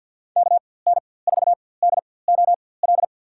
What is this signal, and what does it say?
Morse code, heard as a keyed tone, about 49 wpm, spaced as for 23 wpm, 700 Hz. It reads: KN4DOL
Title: USA Call Signs - 4x Character Spacing - 50wpm